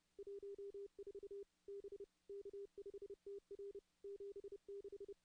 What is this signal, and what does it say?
Morse code, heard as a keyed tone, about 31 wpm, 400 Hz. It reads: 14 B K5TR 76
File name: contests / CW-SS-k5tr- -2100